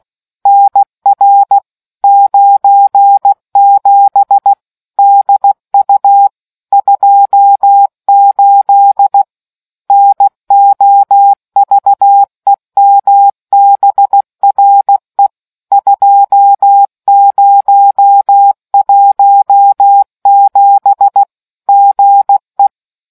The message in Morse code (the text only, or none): NR 97 DU 28 NOVEMBRE 2017 GE